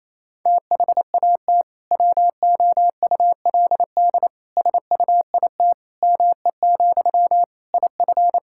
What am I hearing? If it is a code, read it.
Morse code, heard as a keyed tone, about 27 wpm, 700 Hz. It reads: THAT WOULD SUIT ME, IF
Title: Sentences from Top 1600 Words 28wpm